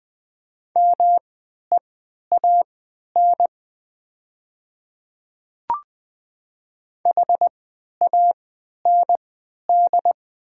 Morse code, heard as a keyed tone, about 20 wpm, spaced as for 11 wpm, 700 Hz. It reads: MEAN HAND
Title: 3x Character Spacing - Top 200 Words - 20wpm